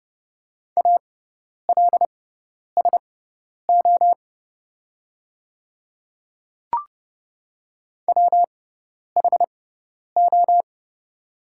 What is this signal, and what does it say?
Morse code, heard as a keyed tone, about 30 wpm, spaced as for 10 wpm, 700 Hz. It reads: ALSO WHO
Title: 6x Character Spacing - Top 200 Words - 30wpm